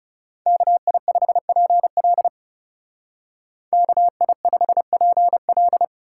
Morse code, heard as a keyed tone, about 36 wpm, 700 Hz. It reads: KI5PL KI5PL
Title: Speed Racing - USA 2x2 Call Signs 30wpm